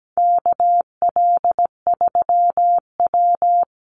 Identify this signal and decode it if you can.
Morse code, heard as a keyed tone, about 17 wpm, 700 Hz. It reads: KL3W
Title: Call Signs Repeated 3x - 17wpm